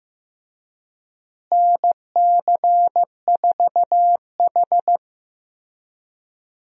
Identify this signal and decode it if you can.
Morse code, heard as a keyed tone, about 15 wpm, 700 Hz. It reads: NC4H
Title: USA Call Signs - 4 Characters - Repeated 3x - 15wpm